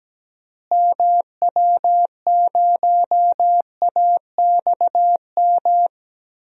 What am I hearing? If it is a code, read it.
Morse code, heard as a keyed tone, about 17 wpm, 700 Hz. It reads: MW0AXM
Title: DXCC-100 Call Signs - 6 Characters - Repeated 2x - 17wpm